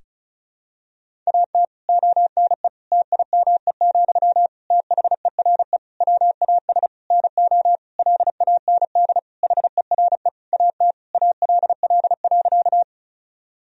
Morse code, heard as a keyed tone, about 36 wpm, 700 Hz. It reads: AT ONE TIME, THERE WAS NO LAND HERE AT ALL.
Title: Sentences from Top 1000 Words - Review Part 1 - 35wpm